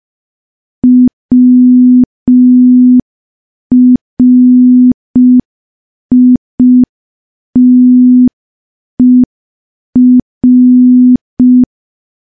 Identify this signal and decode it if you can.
Morse code, heard as a keyed tone, about 5 wpm, 250 Hz. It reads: WRITER
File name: morse / morse-training-05WPM